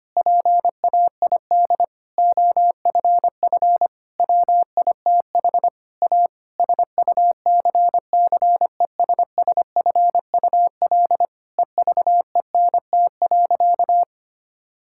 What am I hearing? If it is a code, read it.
Morse code, heard as a keyed tone, about 25 wpm, 700 Hz. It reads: PAID OFF WITH A SUCCESSFUL EVENT.
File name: practice / 25wpm / Difficult Sentences with Hints - 25wpm